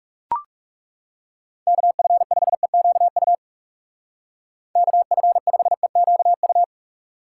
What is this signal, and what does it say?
Morse code, heard as a keyed tone, about 38 wpm, 700 Hz. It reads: KF5EQU KF5EQU